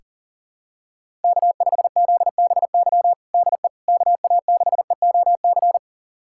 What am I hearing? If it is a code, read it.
Morse code, heard as a keyed tone, about 40 wpm, 700 Hz. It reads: K5ZBY DE KA6EOC
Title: USA Callsign DE Callsign 40wpm